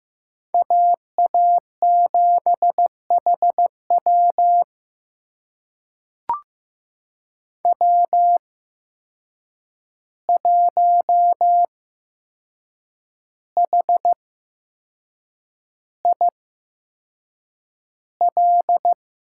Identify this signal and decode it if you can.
Morse code, heard as a keyed tone, about 15 wpm, spaced as for 6 wpm, 700 Hz. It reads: AA7HW W1HIL
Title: USA Call Signs - 5 Characters - 8x Character Spacing - 15wpm